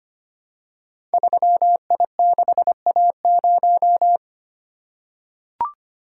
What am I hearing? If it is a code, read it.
Morse code, heard as a keyed tone, about 25 wpm, 700 Hz. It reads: 3I6A0
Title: Code Groups - 5 Letter:Numbers - 25wpm